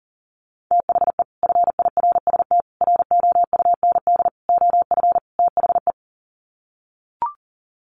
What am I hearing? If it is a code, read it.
Morse code, heard as a keyed tone, about 40 wpm, 700 Hz. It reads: THE FIRST ROUND OF THE
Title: Sets of 5 Words Encore 40wpm